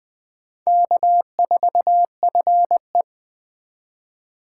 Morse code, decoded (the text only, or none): K4FE